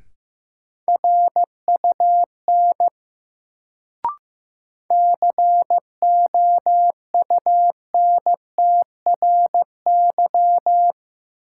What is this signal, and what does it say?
Morse code, heard as a keyed tone, about 15 wpm, 700 Hz. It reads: RUN COUNTRY
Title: Sentences from Top 300 Words - New Word Drill - 15wpm